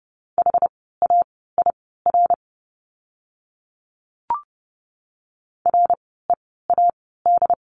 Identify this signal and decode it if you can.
Morse code, heard as a keyed tone, about 30 wpm, spaced as for 17 wpm, 700 Hz. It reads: HAIR READ
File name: Lesson 17f - Letter u - Words 30wpm